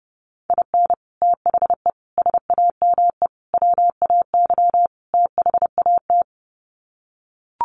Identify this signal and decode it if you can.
Morse code, heard as a keyed tone, about 30 wpm, 700 Hz. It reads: IN THE SAME WAY THAT
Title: Sets of 5 Words 30wpm